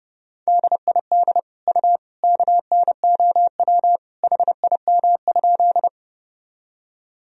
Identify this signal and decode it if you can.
Morse code, heard as a keyed tone, about 32 wpm, 700 Hz. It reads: DID U KNOW HIM?
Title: Sentences from Top 100 Words and Abbreviations 30wpm